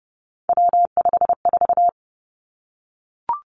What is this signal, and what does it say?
Morse code, heard as a keyed tone, about 30 wpm, 700 Hz. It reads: W54
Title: Three Challenging Letter-Number Combinations 30wpm